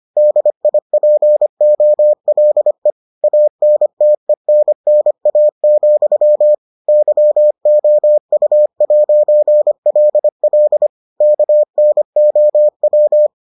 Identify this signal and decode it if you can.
Morse code, heard as a keyed tone, about 25 wpm, 600 Hz. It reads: DIPOLE ANTENNA, YOU'LL KNOW